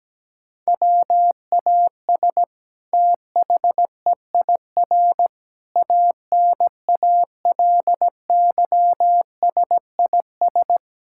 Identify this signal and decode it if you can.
Morse code, heard as a keyed tone, about 17 wpm, 700 Hz. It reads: WAS THEIR ANALYSIS
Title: Sentences from Top 1000 Words - Review Part 5 - 17wpm